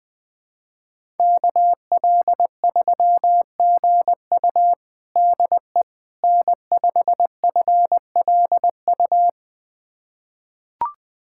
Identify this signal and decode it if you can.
Morse code, heard as a keyed tone, about 20 wpm, 700 Hz. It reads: KL3GU DE N5FLU
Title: USA Callsign DE Callsign - 5 Characters - 20wpm